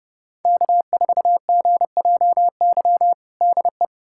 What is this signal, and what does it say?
Morse code, heard as a keyed tone, about 31 wpm, 700 Hz. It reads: K4GJY DE
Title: USA Callsign DE Callsign - 5 Characters - 30wpm